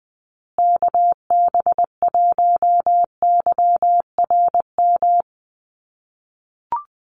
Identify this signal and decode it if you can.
Morse code, heard as a keyed tone, about 20 wpm, 700 Hz. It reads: KB1YRM
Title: USA Call Signs Repeated 2x 20wpm